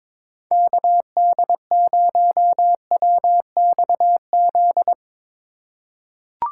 Morse code, heard as a keyed tone, about 22 wpm, 700 Hz. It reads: KD0WXZ
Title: USA Call Signs - 6 Characters - 6x Character Spacing - 22wpm